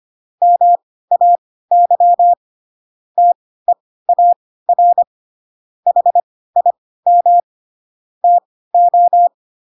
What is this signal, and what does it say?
Morse code, heard as a keyed tone, about 25 wpm, spaced as for 16 wpm, 700 Hz. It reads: MAY TEAR HIM TO
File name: Pinocchio / Pinocchio ch3 10 25 ik1hge